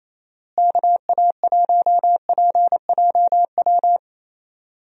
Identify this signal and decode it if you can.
Morse code, heard as a keyed tone, about 28 wpm, 700 Hz. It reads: KA1PJW
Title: USA Call Signs - 6 Characters - Repeated 3x - 28wpm